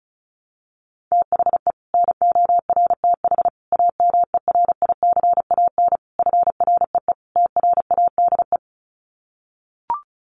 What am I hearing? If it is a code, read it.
Morse code, heard as a keyed tone, about 34 wpm, 700 Hz. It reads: THE NORTH AMERICAN FREE TRADE
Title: Sets of 5 Words Encore 35wpm